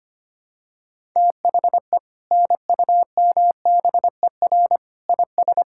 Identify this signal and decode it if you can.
Morse code, heard as a keyed tone, about 25 wpm, 700 Hz. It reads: THE NUMBER IS
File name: practice / 25wpm / Sentences with 2-Word Phrases - 25wpm